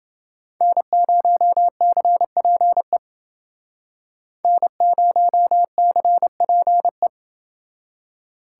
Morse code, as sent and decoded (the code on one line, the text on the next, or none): -. ----- -.-. .--. / -. ----- -.-. .--. .
N0CPE N0CPE